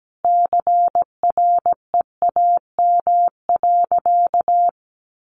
Morse code, heard as a keyed tone, about 17 wpm, 700 Hz. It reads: CREAM.
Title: Sentences from Top 1500 Words 17wpm